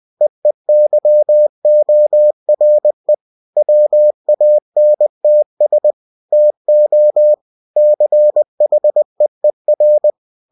Morse code, heard as a keyed tone, about 20 wpm, 600 Hz. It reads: EEYORE WANTS TO CHEER
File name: cw-books / winnie-the-pooh / winnie-20wpm-page-0049